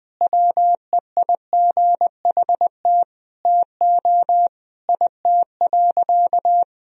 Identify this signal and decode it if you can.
Morse code, heard as a keyed tone, about 20 wpm, 700 Hz. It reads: WEIGHT TO IT.
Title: Sentences from Top 1000 Words - Review Part 3 - 20wpm